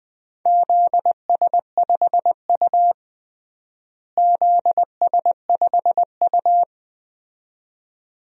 Morse code, heard as a keyed tone, about 20 wpm, 700 Hz. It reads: ZS5U ZS5U
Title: DXCC-100 Call Signs - 4 Characters - Repeated 2x - 20wpm